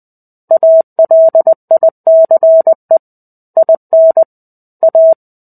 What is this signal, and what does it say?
Morse code, heard as a keyed tone, about 20 wpm, 650 Hz. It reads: ALICE IN A
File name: 05-Chapter-05 20 WPM